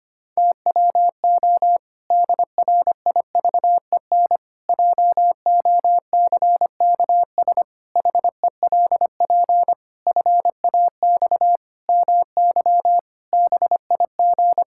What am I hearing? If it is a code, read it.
Morse code, heard as a keyed tone, about 25 wpm, 700 Hz. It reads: TWO DRIVEN JOCKS HELP FAX MY BIG
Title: Pangram Challenge v2 - 25wpm